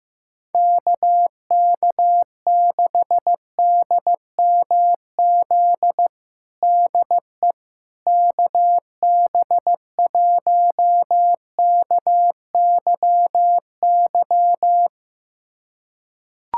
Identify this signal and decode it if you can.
Morse code, heard as a keyed tone, about 15 wpm, 700 Hz. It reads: KK6DMZ DE KB1KYY
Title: USA Callsign DE Callsign - 6 Characters - 15wpm